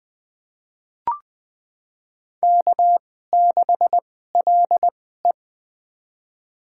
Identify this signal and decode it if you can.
Morse code, heard as a keyed tone, about 20 wpm, spaced as for 14 wpm, 700 Hz. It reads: K6LE
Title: USA Call Signs - 2x Character Spacing - 20wpm